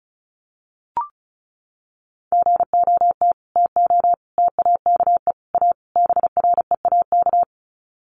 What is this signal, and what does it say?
Morse code, heard as a keyed tone, about 34 wpm, 700 Hz. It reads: GOT TO TAKE A BREAK
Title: Sets of 5 Words Encore 35wpm